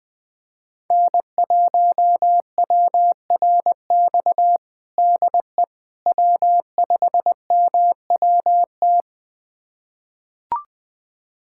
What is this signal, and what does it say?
Morse code, heard as a keyed tone, about 20 wpm, 700 Hz. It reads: N1WRX DE W5MWT
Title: USA Callsign DE Callsign - 5 Characters - 20wpm